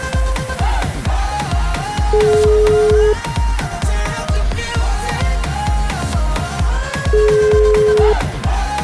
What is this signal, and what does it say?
Ringback tone (Europe, 425 Hz), with noise.